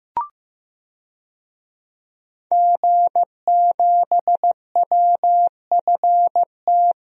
Morse code, heard as a keyed tone, about 15 wpm, 700 Hz. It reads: G7WFT
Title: DXCC-100 Call Signs - 5 Characters - Repeated 2x - 15wpm